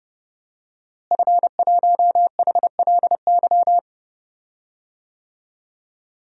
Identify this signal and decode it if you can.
Morse code, heard as a keyed tone, about 30 wpm, 700 Hz. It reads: F1HLY
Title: DXCC-100 Call Signs - 5 Characters - Repeated 3x - 30wpm